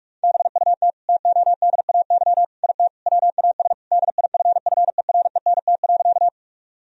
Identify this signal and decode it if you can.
Morse code, heard as a keyed tone, about 41 wpm, 700 Hz. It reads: BUT TODAY IT WAS DIFFERENT.
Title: Sentences from Top 1000 Words - Review Part 4 - 45wpm